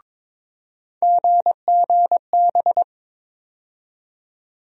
Morse code, heard as a keyed tone, about 22 wpm, 700 Hz. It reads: GGB
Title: USA Call Sign Postfixes - 3 Letters 22wpm